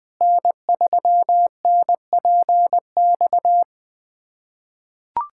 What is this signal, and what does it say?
Morse code, heard as a keyed tone, about 20 wpm, 700 Hz. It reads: N3NPX
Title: USA Callsign DE Callsign - 5 Characters - 20wpm